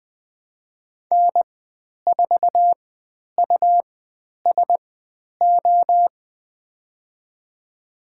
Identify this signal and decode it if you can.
Morse code, heard as a keyed tone, about 20 wpm, spaced as for 10 wpm, 700 Hz. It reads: N4USO
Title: 109 Lesson 18g - Letter m - Call Signs 20wpm